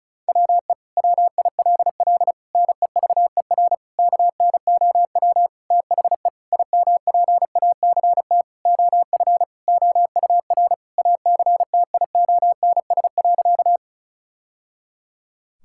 Morse code, heard as a keyed tone, about 34 wpm, 700 Hz. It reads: WE WILL NEVER KNOW THE IMPACT OF OUR ACTIONS.
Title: Sentences from Top 1000 Words - Review Part 6 - 35wpm